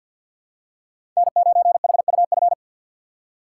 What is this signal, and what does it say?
Morse code, heard as a keyed tone, about 47 wpm, 700 Hz. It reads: N9HUF
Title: USA Call Signs - 2x Character Spacing - 50wpm